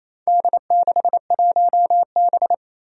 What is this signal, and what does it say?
Morse code, heard as a keyed tone, about 28 wpm, 700 Hz. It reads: D61B
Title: Four Challenging Letter-Number Combinations 28wpm